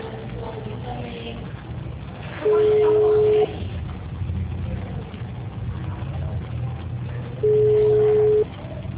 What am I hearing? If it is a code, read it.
Ringback tone (Europe, 425 Hz), with speech.